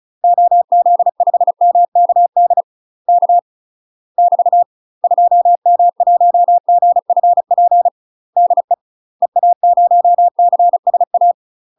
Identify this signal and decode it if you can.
Morse code, heard as a keyed tone, about 36 wpm, 700 Hz. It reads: OZ5MKD K <BT> 2M1GFP DE EA0CSA
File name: UK MASTER QSOs TRAINING / 35WPM / UK MASTER QSOs 0031